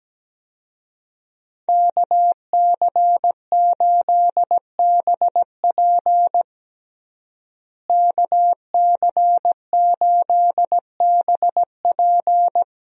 Morse code, heard as a keyed tone, about 17 wpm, 700 Hz. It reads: KC8BP KC8BP
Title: USA Call Signs - 5 Characters - Repeated 2x - 17wpm